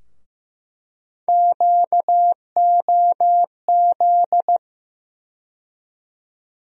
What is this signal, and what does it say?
Morse code, heard as a keyed tone, about 15 wpm, 700 Hz. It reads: QOZ
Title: USA Call Sign Postfixes - 3 Letters 15wpm